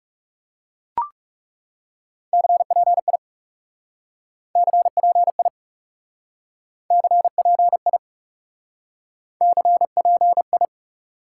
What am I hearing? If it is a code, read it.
Morse code, heard as a keyed tone, about 38 wpm, 700 Hz. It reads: CPI CPI CPI CPI